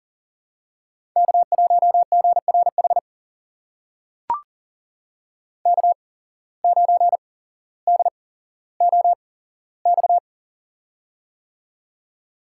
Morse code, heard as a keyed tone, about 40 wpm, spaced as for 17 wpm, 700 Hz. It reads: K1GRH K9DOX
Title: USA Call Signs - 5 Characters - 8x Character Spacing - 40wpm